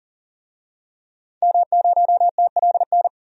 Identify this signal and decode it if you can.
Morse code, heard as a keyed tone, about 40 wpm, 700 Hz. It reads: M0TLN
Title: DXCC-100 Call Signs - 5 Characters - 40wpm